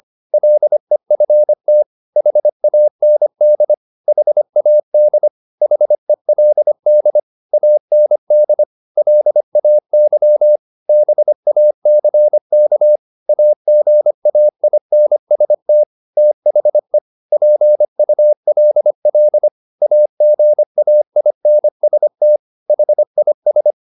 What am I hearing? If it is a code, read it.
Morse code, heard as a keyed tone, about 25 wpm, 600 Hz. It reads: LEFT HAND HAD HELD AND LAY BACK AGAINST THE PULL AGAINST HIS
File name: the old man and the sea part 1 25wpm